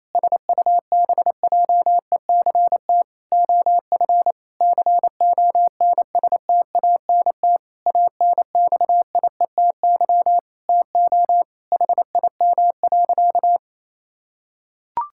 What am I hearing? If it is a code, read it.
Morse code, heard as a keyed tone, about 28 wpm, 700 Hz. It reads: SUBJECT OF CONSTANT ANXIETY TO HIM.
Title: Sentences from Top 1600 Words 28wpm